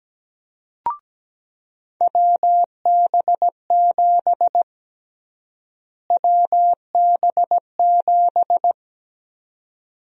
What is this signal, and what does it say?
Morse code, heard as a keyed tone, about 17 wpm, 700 Hz. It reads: WB7 WB7
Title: USA Call Sign Prefixes - Repeated 2x - 17wpm